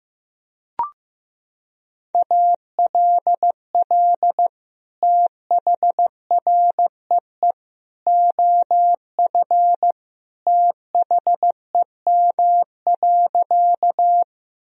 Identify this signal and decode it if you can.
Morse code, heard as a keyed tone, about 15 wpm, 700 Hz. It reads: ALL THREE OF THEM.
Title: Sentences from Top 1000 Words - Review Part 1 - 15wpm